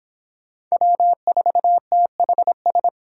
Morse code, heard as a keyed tone, about 25 wpm, 700 Hz. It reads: W4THS